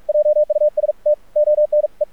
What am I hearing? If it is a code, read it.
Morse code, heard as a keyed tone, about 46 wpm, spaced as for 40 wpm, 600 Hz, with radio noise. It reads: JUST ONE
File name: archived / CRSnet